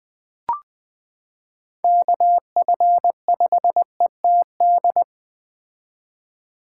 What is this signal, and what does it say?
Morse code, heard as a keyed tone, about 20 wpm, 700 Hz. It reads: KF5ETD
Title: K1USN SST QSO Practice - 20wpm